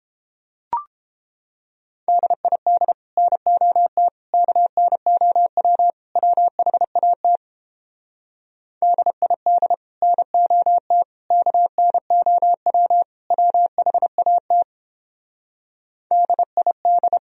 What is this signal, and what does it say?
Morse code, heard as a keyed tone, about 29 wpm, 700 Hz. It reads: DID NOT KNOW WHAT DID NOT KNOW WHAT DID